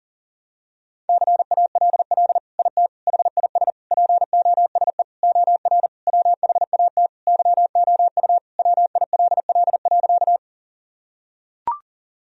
Morse code, heard as a keyed tone, about 40 wpm, 700 Hz. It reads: CALL IT HIS POSE OR WHAT YOU WILL.
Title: Sentences from Top 1000 Words - Review Part 2 - 40wpm